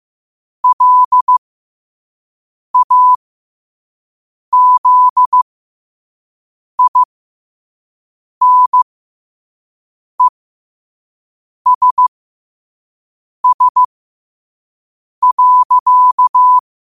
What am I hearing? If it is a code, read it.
Morse code, heard as a keyed tone, about 15 wpm, spaced as for 5 wpm, 1000 Hz. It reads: LAZINESS.
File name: 5wpm